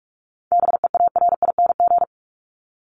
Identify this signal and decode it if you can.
Morse code, heard as a keyed tone, about 44 wpm, 700 Hz. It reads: BEARING
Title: ING Suffix - Top 100 Words with Base Words 45wpm